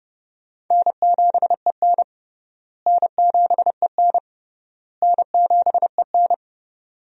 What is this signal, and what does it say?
Morse code, heard as a keyed tone, about 32 wpm, 700 Hz. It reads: N7EN N7EN N7EN
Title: USA Call Signs - 4 Characters - Repeated 3x - 30wpm